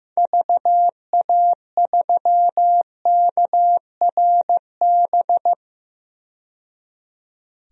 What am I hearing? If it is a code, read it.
Morse code, heard as a keyed tone, about 15 wpm, 700 Hz. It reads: VA3KRB
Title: DXCC-100 Call Signs - 6 Characters - 15wpm